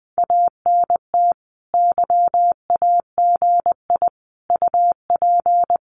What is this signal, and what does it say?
Morse code, heard as a keyed tone, about 20 wpm, 700 Hz. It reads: ANT YAGI UP
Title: Common QSO Phrases Encore 20wpm